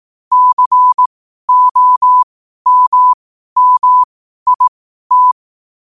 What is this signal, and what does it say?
Morse code, heard as a keyed tone, about 18 wpm, spaced as for 12 wpm, 1000 Hz. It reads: COMMIT
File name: qotdfiles / 12wpm